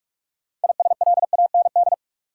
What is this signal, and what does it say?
Morse code, heard as a keyed tone, about 47 wpm, 700 Hz. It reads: ISLAND
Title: Sentences from Top 500 Words - New Word Drill - 45wpm